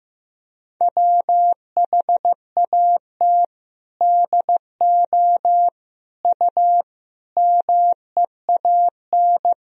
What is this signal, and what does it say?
Morse code, heard as a keyed tone, about 15 wpm, 700 Hz. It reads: WHAT DO U MEAN
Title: Sentences from Top 200 Words and Abbreviations 15wpm